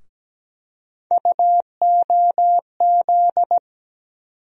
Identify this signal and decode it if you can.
Morse code, heard as a keyed tone, about 17 wpm, 700 Hz. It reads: UOZ
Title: USA Call Sign Postfixes - 3 Letters 17wpm